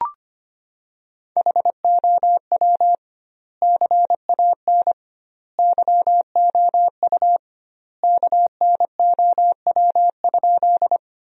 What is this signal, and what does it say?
Morse code, heard as a keyed tone, about 25 wpm, 700 Hz. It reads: HOW CAN YOU KNOW?
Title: 2x Word-Spacing - Sentences from Top 100 Words - 25wpm